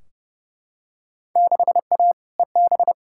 Morse code, heard as a keyed tone, about 30 wpm, 700 Hz. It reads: 6A EB
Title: Field Day Exchanges Encore - 30wpm